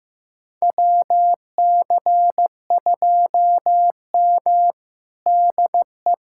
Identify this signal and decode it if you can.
Morse code, heard as a keyed tone, about 15 wpm, 700 Hz. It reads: WC2M DE
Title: USA Callsign DE Callsign - 4 Characters - 15wpm